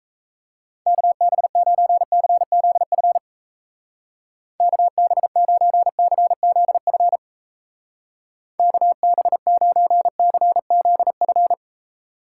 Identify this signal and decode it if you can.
Morse code, heard as a keyed tone, about 36 wpm, 700 Hz. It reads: KB9CZF KB9CZF KB9CZF